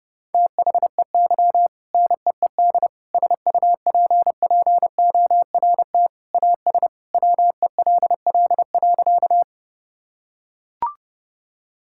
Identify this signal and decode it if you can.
Morse code, heard as a keyed tone, about 30 wpm, 700 Hz. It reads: THEY NEED SUPPORT AS WELL.
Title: Sentences from Top 1000 Words - Review Part 5 - 30wpm